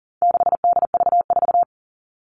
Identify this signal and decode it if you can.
Morse code, heard as a keyed tone, about 40 wpm, 700 Hz. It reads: 6DV4
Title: All Cut Numbers - 4-Digit 40wpm